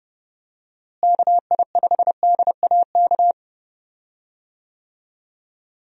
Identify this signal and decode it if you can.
Morse code, heard as a keyed tone, about 32 wpm, 700 Hz. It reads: KI5DAK
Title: USA Call Signs - 6 Characters - Repeated 3x - 30wpm